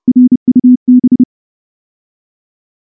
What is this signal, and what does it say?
Morse code, heard as a keyed tone, about 30 wpm, 250 Hz. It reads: RUB
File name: morse-training-30WPM